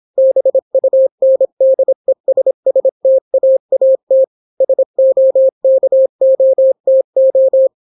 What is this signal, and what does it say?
Morse code, heard as a keyed tone, about 26 wpm, 530 Hz. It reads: BUNDESSTAAT SOKOTO